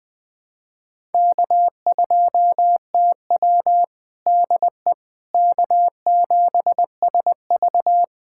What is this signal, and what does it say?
Morse code, heard as a keyed tone, about 20 wpm, 700 Hz. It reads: K2TW DE K7SV
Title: K1USN SST QSO Practice 20wpm